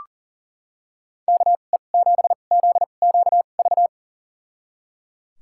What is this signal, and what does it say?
Morse code, heard as a keyed tone, about 38 wpm, spaced as for 28 wpm, 700 Hz. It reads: KE7ZQV
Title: USA Call Signs - 2x Character Spacing - 40wpm